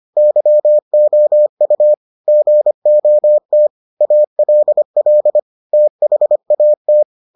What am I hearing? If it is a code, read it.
Morse code, heard as a keyed tone, about 25 wpm, 600 Hz. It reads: YOU GOT ALL THAT